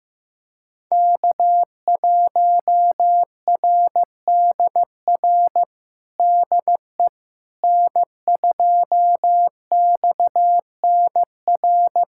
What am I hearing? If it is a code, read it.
Morse code, heard as a keyed tone, about 15 wpm, 700 Hz. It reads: K1RDR DE N2XNR
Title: USA Callsign DE Callsign - 5 Characters - 15wpm